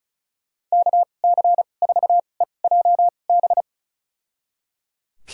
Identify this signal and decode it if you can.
Morse code, heard as a keyed tone, about 34 wpm, spaced as for 25 wpm, 700 Hz. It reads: KC4EJB
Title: USA Call Signs - 2x Character Spacing - 35wpm